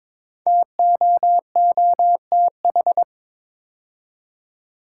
Morse code, heard as a keyed tone, about 22 wpm, 700 Hz. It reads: TOOTH
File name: 5 Letter Words - First 3 Letters Given - 22wpm